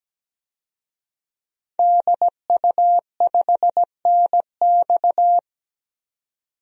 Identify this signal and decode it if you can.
Morse code, heard as a keyed tone, about 17 wpm, 700 Hz. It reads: DU5NX
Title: DXCC-100 Call Signs - 5 Characters - 17wpm